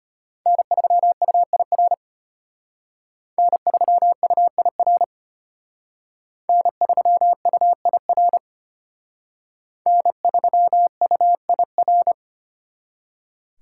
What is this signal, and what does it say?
Morse code, heard as a keyed tone, about 32 wpm, 700 Hz. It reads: N3UIR N3UIR N3UIR N3UIR